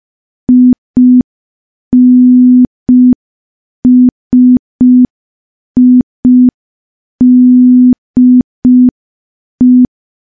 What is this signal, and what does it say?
Morse code, heard as a keyed tone, about 5 wpm, 250 Hz. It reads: INSIDE
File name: morse / morse-training-05WPM